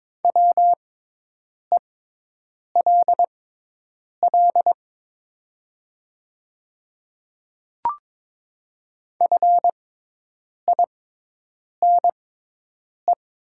Morse code, heard as a keyed tone, about 22 wpm, spaced as for 8 wpm, 700 Hz. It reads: WELL FINE
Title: 6x Character Spacing - Top 500 Words - 22wpm